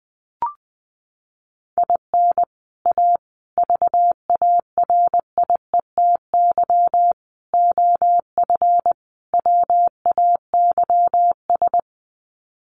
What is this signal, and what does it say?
Morse code, heard as a keyed tone, about 20 wpm, 700 Hz. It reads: IN A VARIETY OF WAYS
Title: Sets of 5 Words Encore 20wpm